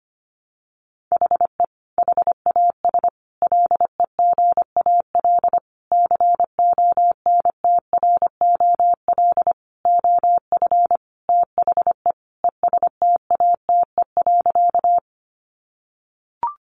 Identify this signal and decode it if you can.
Morse code, heard as a keyed tone, about 25 wpm, 700 Hz. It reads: HE HAS LEGAL CONTROL OF THE ESTATE.
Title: Sentences from Top 1200 Words 25wpm